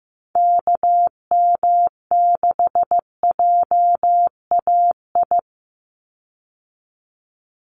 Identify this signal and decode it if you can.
Morse code, heard as a keyed tone, about 15 wpm, 700 Hz. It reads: KM6JAI
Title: USA Call Signs 15wpm